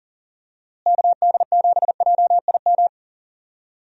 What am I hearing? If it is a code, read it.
Morse code, heard as a keyed tone, about 40 wpm, 700 Hz. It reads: KD7JIM
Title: USA Call Signs - 6 Characters - Repeated 3x - 40wpm